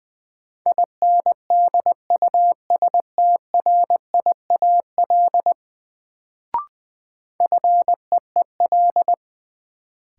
Morse code, heard as a keyed tone, about 20 wpm, 700 Hz. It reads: INDUSTRIAL FEEL
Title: Sets of 5 Words - New Word Drill - 20wpm